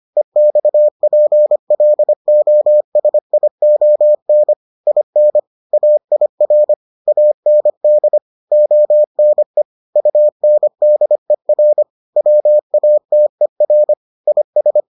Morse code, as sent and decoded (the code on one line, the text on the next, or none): . -..- .--. .-.. --- ... .. --- -. / .. -. / .- .. .-. / .- -. -.. / --- -. / ..- -. -.. .-. / .-- .- - . .-. / .. ...
EXPLOSION IN AIR AND ONE UNDER WATER IS